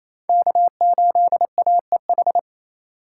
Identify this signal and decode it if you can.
Morse code, heard as a keyed tone, about 28 wpm, 700 Hz. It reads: K8AEH
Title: USA Call Signs - 5 Characters - 2x Character Spacing - 28wpm